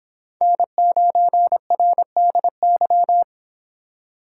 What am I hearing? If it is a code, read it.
Morse code, heard as a keyed tone, about 25 wpm, 700 Hz. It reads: N9RDY